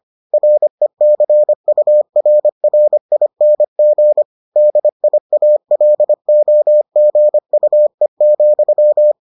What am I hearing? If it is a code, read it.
Morse code, heard as a keyed tone, about 25 wpm, 600 Hz. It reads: RECURRING DIALOGUE,